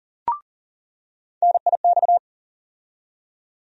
Morse code, heard as a keyed tone, about 40 wpm, 700 Hz. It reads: NIX
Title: USA Call Sign Postfixes - 3 Letters 40wpm